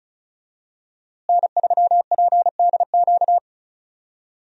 Morse code, heard as a keyed tone, about 34 wpm, 700 Hz. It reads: N3PDQ
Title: USA Call Signs - 5 Characters - Repeated 2x - 35wpm